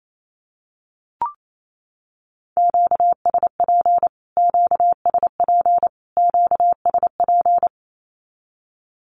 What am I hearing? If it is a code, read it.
Morse code, heard as a keyed tone, about 28 wpm, 700 Hz. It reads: QSP QSP QSP
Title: Q-Codes Repeated 3x 28wpm